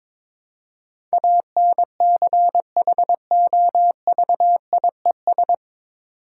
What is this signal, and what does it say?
Morse code, heard as a keyed tone, about 22 wpm, 690 Hz. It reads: ANCHOVIES
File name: Difficult Words with Hints - 22wpm